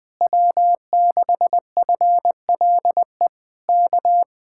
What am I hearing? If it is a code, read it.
Morse code, heard as a keyed tone, about 20 wpm, 700 Hz. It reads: W6FLE K
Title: Speed-Racing Common QSO Phrases - 20wpm